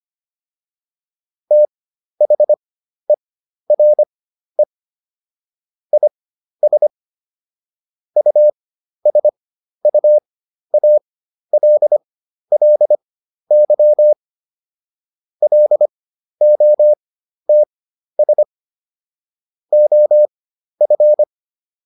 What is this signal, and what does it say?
Morse code, heard as a keyed tone, about 25 wpm, spaced as for 12 wpm, 600 Hz. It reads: THERE IS USUALLY LOTS OF